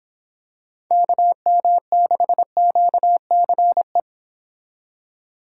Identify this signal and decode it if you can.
Morse code, heard as a keyed tone, about 26 wpm, 700 Hz. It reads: KM6QCE